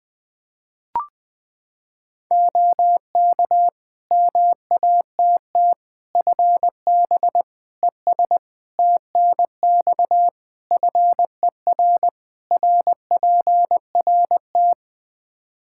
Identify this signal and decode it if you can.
Morse code, heard as a keyed tone, about 20 wpm, 700 Hz. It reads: OK MATT FB ES TNX FER RPRT
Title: Practice QSO - Exchange 2 - Info - 20wpm